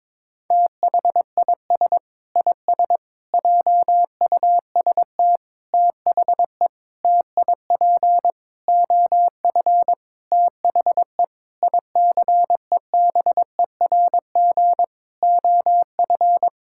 Morse code, heard as a keyed tone, about 22 wpm, 700 Hz. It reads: THIS IS JUST THE TIP OF THE ICEBERG OF
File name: practice / 22wpm / Sentences with 8-Word Phrases - 22wpm